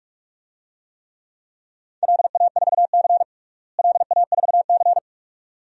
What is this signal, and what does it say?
Morse code, heard as a keyed tone, about 46 wpm, 700 Hz. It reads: LA4C LA4C
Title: DXCC-100 Call Signs - 4 Characters - Repeated 2x - 45wpm